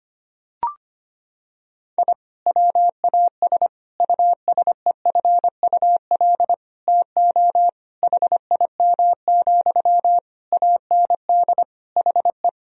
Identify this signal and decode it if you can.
Morse code, heard as a keyed tone, about 25 wpm, 700 Hz. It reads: I WAS USEFUL TO HIM, AND HE
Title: Sentences from Top 1000 Words - Review Part 4 - 25wpm